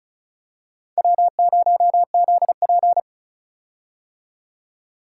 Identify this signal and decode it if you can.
Morse code, heard as a keyed tone, about 34 wpm, 700 Hz. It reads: W0ZP
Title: K1USN SST QSO Practice - 35wpm